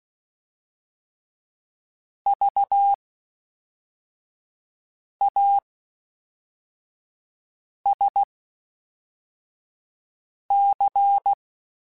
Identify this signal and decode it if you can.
Morse code, heard as a keyed tone, about 16 wpm, spaced as for 4 wpm, 800 Hz. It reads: VASC
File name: Special INGs / Special INGs-mp3s / Fast ing @10